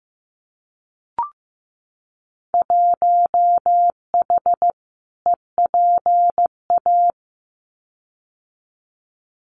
Morse code, heard as a keyed tone, about 15 wpm, 700 Hz. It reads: 1H EPA
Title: Winter Field Day Exchanges 15wpm